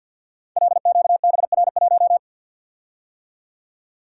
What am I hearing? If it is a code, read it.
Morse code, heard as a keyed tone, about 47 wpm, 700 Hz. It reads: LQBR1
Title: Code Groups - 5 Letter:Numbers - 50wpm